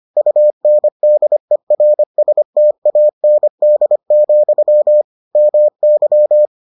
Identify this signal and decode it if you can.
Morse code, heard as a keyed tone, about 25 wpm, 600 Hz. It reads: UNDERSTAND, MY